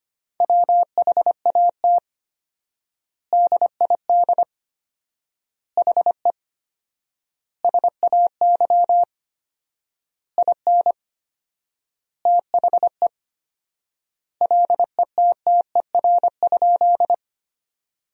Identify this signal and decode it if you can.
Morse code, heard as a keyed tone, about 25 wpm, 700 Hz. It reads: WHAT DID HE SAY IN THE LETTER?
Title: 4x Word-Spacing - Sentences from Top 300 Words - 25wpm